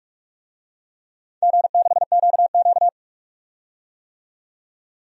Morse code, heard as a keyed tone, about 44 wpm, 700 Hz. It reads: G6QQ
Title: DXCC-100 Call Signs - 4 Characters - 45wpm